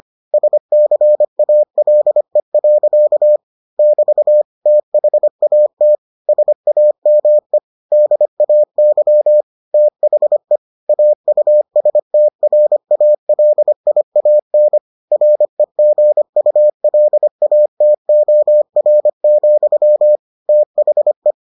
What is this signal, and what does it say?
Morse code, heard as a keyed tone, about 25 wpm, 600 Hz. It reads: SCALE. <BT> THAT SAME DAY THE AUSTRALIAN REGULATOR, THE